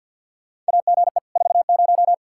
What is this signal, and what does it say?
Morse code, heard as a keyed tone, about 51 wpm, 700 Hz. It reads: AGE 40
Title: K1USN SST QSO Practice 50wpm